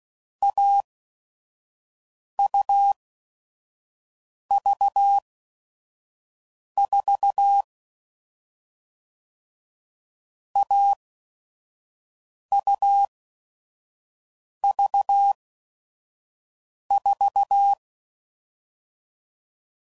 Morse code, heard as a keyed tone, about 16 wpm, spaced as for 5 wpm, 800 Hz. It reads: AUV4 AUV4
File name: Combinations / Difficult Combos / AUV AUV4